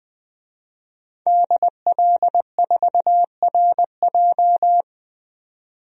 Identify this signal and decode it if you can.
Morse code, heard as a keyed tone, about 20 wpm, 700 Hz. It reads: DL4RJ
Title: DXCC-100 Call Signs - 5 Characters - 20wpm